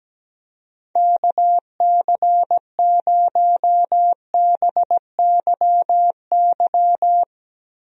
Morse code, heard as a keyed tone, about 17 wpm, 700 Hz. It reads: KC0BYY
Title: USA Call Signs - 6 Characters - Repeated 2x - 17wpm